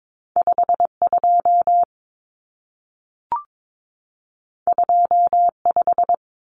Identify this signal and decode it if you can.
Morse code, heard as a keyed tone, about 22 wpm, 700 Hz. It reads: 52 25
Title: Numbers - 2-Digit 22wpm